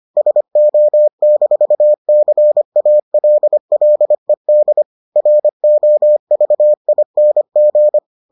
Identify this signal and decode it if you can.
Morse code, heard as a keyed tone, about 25 wpm, 600 Hz. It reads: SO-CALLED ROVING